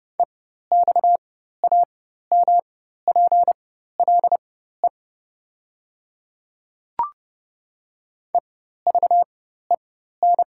Morse code, heard as a keyed tone, about 30 wpm, spaced as for 14 wpm, 700 Hz. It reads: EXAMPLE EVEN
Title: 4x Character Spacing - Top 300 Words - 30wpm